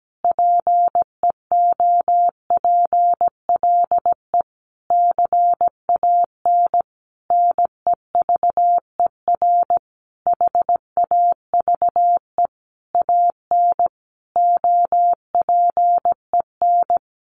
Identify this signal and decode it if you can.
Morse code, heard as a keyed tone, about 17 wpm, 700 Hz. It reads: PEOPLE CAN NEVER HAVE AN OPEN
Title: Sentences from Top 1400 Words 17wpm